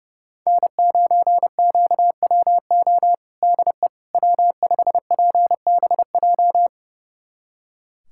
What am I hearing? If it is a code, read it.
Morse code, heard as a keyed tone, about 30 wpm, 700 Hz. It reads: N9QWO DE W5PBJ
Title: USA Callsign DE Callsign - 5 Characters - 30wpm